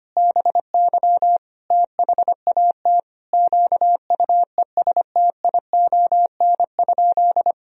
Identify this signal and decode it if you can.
Morse code, heard as a keyed tone, about 25 wpm, 700 Hz. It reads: BY THAT QUESTION?
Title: Sentences from Top 400 Words and Abbreviations 25wpm